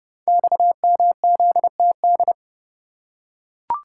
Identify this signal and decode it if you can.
Morse code, heard as a keyed tone, about 30 wpm, 700 Hz. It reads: XMZTD
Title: Code Groups - 5 Letter:Numbers - 30wpm